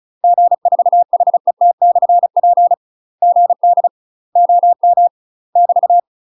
Morse code, heard as a keyed tone, about 34 wpm, 700 Hz. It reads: G4HET/P GD OM <BT>
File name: UK MASTER QSOs TRAINING / 35WPM / UK MASTER QSOs 0037